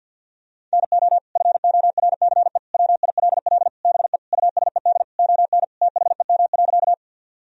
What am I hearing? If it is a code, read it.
Morse code, heard as a keyed tone, about 50 wpm, 700 Hz. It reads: NO FORCE WILL BE USED ON THEM.
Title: Sentences from Top 1000 Words - Review Part 3 - 50wpm